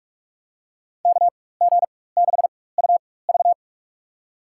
Morse code, heard as a keyed tone, about 46 wpm, spaced as for 21 wpm, 700 Hz. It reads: KG6UV
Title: USA Call Signs - 5 Characters - 4x Character Spacing - 45wpm